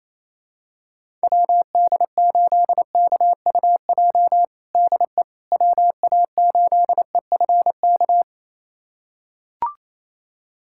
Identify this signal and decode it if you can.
Morse code, heard as a keyed tone, about 27 wpm, 700 Hz. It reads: WD8KUJ DE WA8EFK
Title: USA Callsign DE Callsign 28wpm